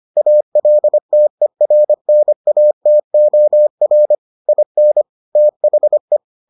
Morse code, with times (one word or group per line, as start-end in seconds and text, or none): 0.2-4.2 ALTERNATOR
4.5-5.0 IN
5.4-6.2 THE